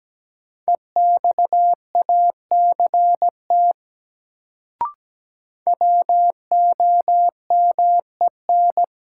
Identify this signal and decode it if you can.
Morse code, heard as a keyed tone, about 17 wpm, 700 Hz. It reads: EXACT WOMEN
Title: Sentences from Top 1300 Words - New Word Drill Alt - 17wpm